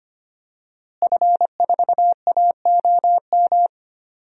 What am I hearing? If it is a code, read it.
Morse code, heard as a keyed tone, about 25 wpm, 700 Hz. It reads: F4AOM
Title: DXCC-100 Call Signs - 5 Characters - Repeated 2x - 25wpm